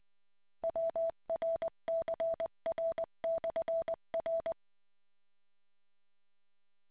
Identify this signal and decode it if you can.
Morse code, heard as a keyed tone, about 20 wpm, 660 Hz. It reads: WRCR/R